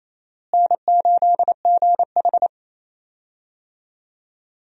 Morse code, heard as a keyed tone, about 28 wpm, 700 Hz. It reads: N8GH
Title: USA Call Signs - 4 Characters - Repeated 3x - 28wpm